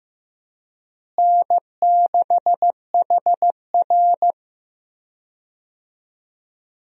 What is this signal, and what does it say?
Morse code, heard as a keyed tone, about 15 wpm, 700 Hz. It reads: N6HR
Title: USA Call Signs - 4 Characters - Repeated 2x - 15wpm